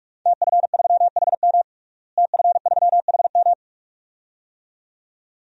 Morse code, heard as a keyed tone, about 41 wpm, 700 Hz. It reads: TF3HM TF3HM
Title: DXCC-100 Call Signs - 5 Characters - Repeated 2x - 45wpm